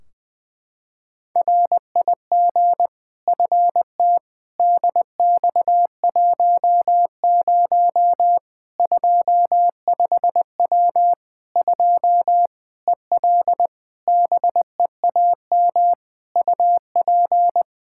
Morse code, heard as a keyed tone, about 21 wpm, 700 Hz. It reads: RIG FT DX10 25W 2 EL BEAM UP
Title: Practice QSO - Exchange 2 - Info - 20wpm